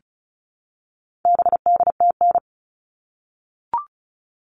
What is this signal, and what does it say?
Morse code, heard as a keyed tone, about 34 wpm, 700 Hz. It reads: BDTN
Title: All Cut Numbers - 4-Digit 35wpm